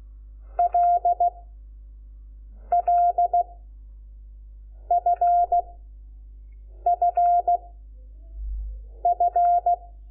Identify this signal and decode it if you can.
Morse code, heard as a keyed tone, about 16 wpm, 700 Hz. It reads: L L F F F
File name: Lecciones / CLASE Nº5